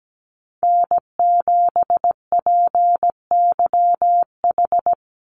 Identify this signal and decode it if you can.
Morse code, heard as a keyed tone, about 17 wpm, 700 Hz. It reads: N7PYH
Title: USA Call Signs 17wpm